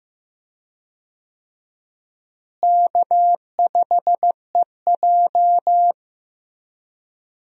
Morse code, heard as a keyed tone, about 15 wpm, 700 Hz. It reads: K5EJ
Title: USA Call Signs - 4 Characters - Repeated 3x - 15wpm